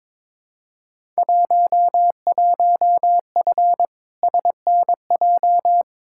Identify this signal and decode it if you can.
Morse code, heard as a keyed tone, about 22 wpm, 700 Hz. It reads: 11F SNJ
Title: Field Day Exchanges Encore - 22wpm